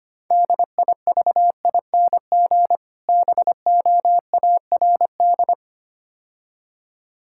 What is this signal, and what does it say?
Morse code, heard as a keyed tone, about 25 wpm, 700 Hz. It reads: DIVING BOARD
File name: practice / 25wpm / Difficult Words - Speed-Racing with Hints - 25wpm